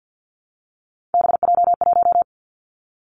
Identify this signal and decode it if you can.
Morse code, heard as a keyed tone, about 49 wpm, 700 Hz. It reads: BJ1
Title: Three Challenging Letter-Number Combinations 50wpm